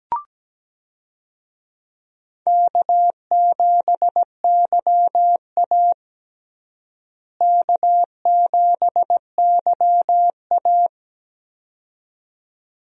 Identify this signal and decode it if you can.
Morse code, heard as a keyed tone, about 17 wpm, 700 Hz. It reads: K7YA K7YA
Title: USA Call Signs - 4 Characters - Repeated 2x - 17wpm